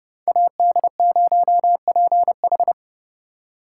Morse code, heard as a keyed tone, about 27 wpm, 700 Hz. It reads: AD0PH